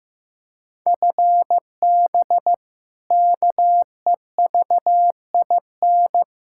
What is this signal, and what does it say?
Morse code, heard as a keyed tone, about 16 wpm, 700 Hz. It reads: FB KEVIN
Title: Practice QSO - Exchange 3 - More Info - 15wpm